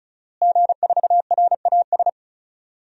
Morse code, heard as a keyed tone, about 34 wpm, 700 Hz. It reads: G4RAS
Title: DXCC-100 Call Signs - 5 Characters - Repeated 2x - 35wpm